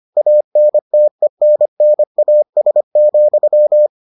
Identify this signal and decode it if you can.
Morse code, heard as a keyed tone, about 25 wpm, 600 Hz. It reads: ANTENNAS,